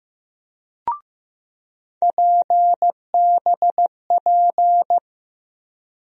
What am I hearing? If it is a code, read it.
Morse code, heard as a keyed tone, about 15 wpm, 700 Hz. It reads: PBP
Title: USA Call Sign Postfixes - 3 Letters 15wpm